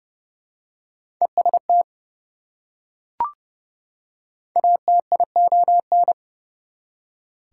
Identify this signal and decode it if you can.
Morse code, heard as a keyed tone, about 30 wpm, 700 Hz. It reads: EST ATION
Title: Prefixes and Suffixes 30wpm